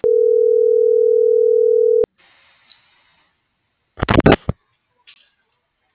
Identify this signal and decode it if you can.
Ringback tone (North America), 440 + 480 Hz.